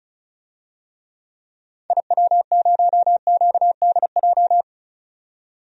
Morse code, heard as a keyed tone, about 36 wpm, 700 Hz. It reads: IW0QDJ
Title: DXCC-100 Call Signs - 6 Characters - 35wpm